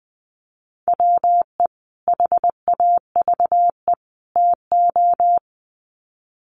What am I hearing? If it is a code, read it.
Morse code, heard as a keyed tone, about 20 wpm, 700 Hz. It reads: WE HAVE TO
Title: Sets of 3 Words 20wpm